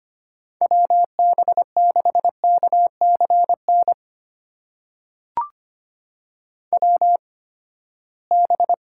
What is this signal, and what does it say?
Morse code, heard as a keyed tone, about 25 wpm, spaced as for 17 wpm, 700 Hz. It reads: WB6KCN WB
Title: USA Call Signs - 6 Characters - 8x Character Spacing - 25wpm